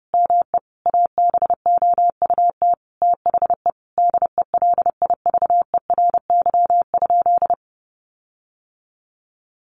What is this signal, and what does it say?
Morse code, heard as a keyed tone, about 30 wpm, 700 Hz. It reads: ME ABOUT THE DELIVERY?
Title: Sentences from Top 1500 Words 30wpm